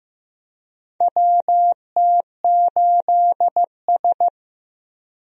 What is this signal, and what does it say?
Morse code, heard as a keyed tone, about 15 wpm, 700 Hz. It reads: WT8S
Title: USA Call Signs - 4 Characters - 15wpm